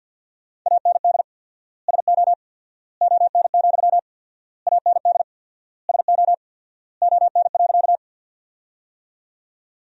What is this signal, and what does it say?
Morse code, heard as a keyed tone, about 53 wpm, 700 Hz. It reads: AND SO ON, AND SO ON.
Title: 4x Word-Spacing - Sentences from Top 100 Words - 50wpm